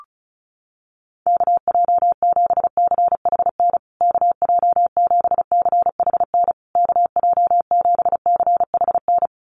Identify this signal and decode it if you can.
Morse code, heard as a keyed tone, about 34 wpm, 700 Hz. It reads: KJ7CHN KJ7CHN KJ7CHN
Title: USA Call Signs Repeated 3x 35wpm